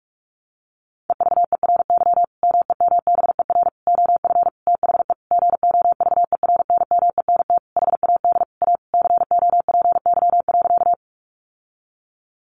Morse code, heard as a keyed tone, about 44 wpm, 700 Hz. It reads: EVERY MEMBER OF THE GOVERNMENT HAD A COPY.
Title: Sentences from Top 1400 Words 45wpm